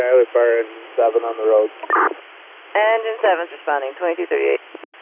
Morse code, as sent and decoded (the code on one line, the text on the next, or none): none